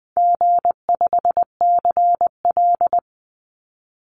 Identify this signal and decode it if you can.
Morse code, heard as a keyed tone, about 20 wpm, 700 Hz. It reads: G5CL
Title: Easy Call Signs Repeated 2x - 20wpm